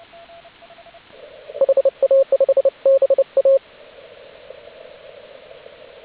Morse code, heard as a keyed tone, about 29 wpm, 540 Hz, with radio noise. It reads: HA5BA